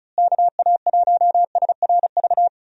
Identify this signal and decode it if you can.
Morse code, heard as a keyed tone, about 36 wpm, 700 Hz. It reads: KA1SRV
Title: USA Call Signs - 6 Characters - Repeated 3x - 35wpm